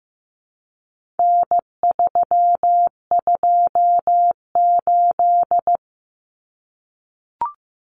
Morse code, heard as a keyed tone, about 15 wpm, 700 Hz. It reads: N328
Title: Common Cut Numbers - 4-Digit 15wpm